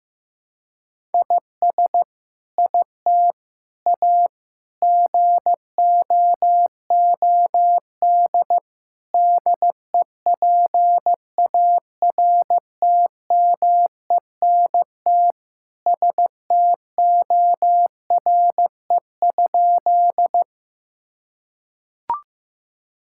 Morse code, heard as a keyed tone, about 15 wpm, 700 Hz. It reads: IS IT A GOOD DEPARTMENT STORE?
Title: Sentences from Top 1000 Words - Review Part 5 - 15wpm